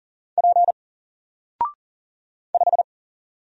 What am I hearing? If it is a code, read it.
Morse code, heard as a keyed tone, about 40 wpm, 700 Hz. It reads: P 5
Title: Single Letter-Number 40wpm